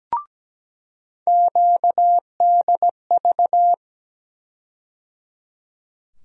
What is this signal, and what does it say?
Morse code, heard as a keyed tone, about 17 wpm, 700 Hz. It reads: QDV
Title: USA Call Sign Postfixes - 3 Letters 17wpm